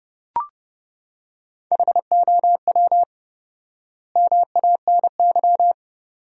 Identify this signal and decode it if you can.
Morse code, heard as a keyed tone, about 30 wpm, 700 Hz. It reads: HOW MANY
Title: 4x Word-Spacing - Sentences from Top 400 Words - 30wpm